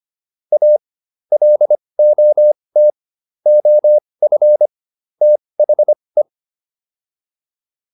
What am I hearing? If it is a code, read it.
Morse code, heard as a keyed tone, about 25 wpm, spaced as for 20 wpm, 600 Hz. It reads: A LOT OF THE